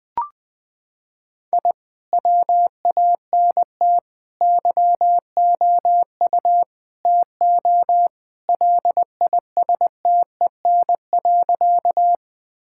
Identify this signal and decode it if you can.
Morse code, heard as a keyed tone, about 20 wpm, 700 Hz. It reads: I WANT YOU TO LISTEN.
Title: Sentences with 4-Word Phrases - 20wpm